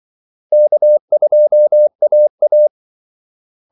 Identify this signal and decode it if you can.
Morse code, heard as a keyed tone, about 24 wpm, 600 Hz. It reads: K2AA
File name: Callsigns-24wpm